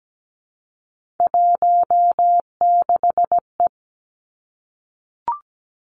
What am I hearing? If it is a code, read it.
Morse code, heard as a keyed tone, about 17 wpm, 700 Hz. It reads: 16E
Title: Three Challenging Letter-Number Combinations 17wpm